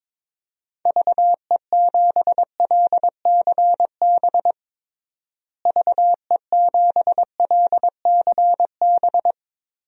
Morse code, heard as a keyed tone, about 22 wpm, 700 Hz. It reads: VE7LCB VE7LCB
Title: DXCC-100 Call Signs - 6 Characters - Repeated 3x - 22wpm